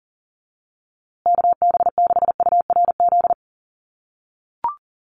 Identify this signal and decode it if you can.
Morse code, heard as a keyed tone, about 40 wpm, 700 Hz. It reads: KB6URZ
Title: USA Call Signs Repeated 3x 40wpm